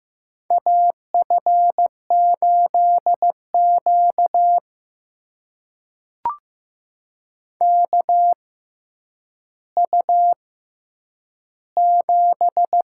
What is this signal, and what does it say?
Morse code, heard as a keyed tone, about 15 wpm, spaced as for 9 wpm, 700 Hz. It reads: AF8Q KU7
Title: USA Call Signs - 4 Characters - 6x Character Spacing - 15wpm